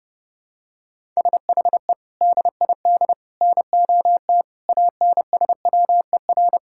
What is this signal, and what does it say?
Morse code, heard as a keyed tone, about 32 wpm, 700 Hz. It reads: SHE DID NOT ANSWER
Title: Sentences from Top 1000 Words - Review Part 5 - 30wpm